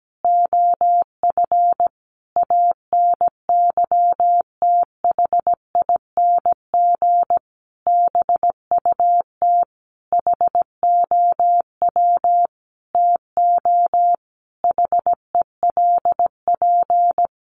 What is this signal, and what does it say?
Morse code, heard as a keyed tone, about 17 wpm, 700 Hz. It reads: OF ANYTHING BUT HOW TO HELP
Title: Sentences from Top 1000 Words 17wpm